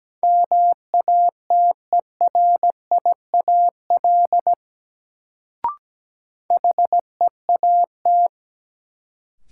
Morse code, heard as a keyed tone, about 17 wpm, 700 Hz. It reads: MATERIAL HEAT
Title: Sentences from Top 1100 Words - New Word Drill Alt - 17wpm